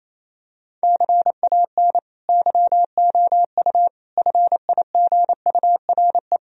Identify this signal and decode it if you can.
Morse code, heard as a keyed tone, about 28 wpm, 700 Hz. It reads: CAN YOU FIGURE
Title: Sentences from Top 1000 Words - Review Part 2 - 28wpm